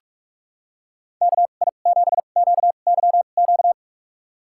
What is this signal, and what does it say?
Morse code, heard as a keyed tone, about 46 wpm, spaced as for 33 wpm, 700 Hz. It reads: KI7QYQ
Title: USA Call Signs - 6 Characters - 2x Character Spacing - 45wpm